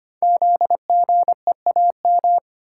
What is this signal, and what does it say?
Morse code, heard as a keyed tone, about 25 wpm, 700 Hz. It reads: ZGEAM
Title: Code Groups - 5 Letters - 25wpm